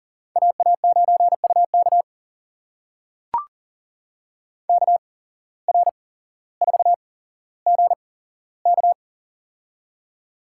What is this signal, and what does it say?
Morse code, heard as a keyed tone, about 40 wpm, spaced as for 17 wpm, 700 Hz. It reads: AA9UK KR4GK
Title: USA Call Signs - 5 Characters - 8x Character Spacing - 40wpm